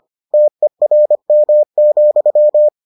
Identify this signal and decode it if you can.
Morse code, heard as a keyed tone, about 25 wpm, 600 Hz. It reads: TERM,